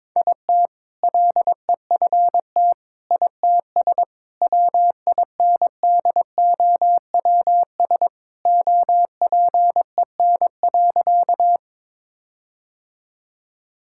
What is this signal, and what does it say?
Morse code, heard as a keyed tone, about 22 wpm, 700 Hz. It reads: IT LEFT ITS WINDOWS OPEN.
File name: All-Age Appropriate Jokes - 22wpm